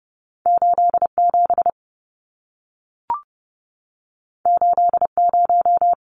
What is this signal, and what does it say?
Morse code, heard as a keyed tone, about 30 wpm, 700 Hz. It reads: 87 80
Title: Numbers - 2-Digit 30wpm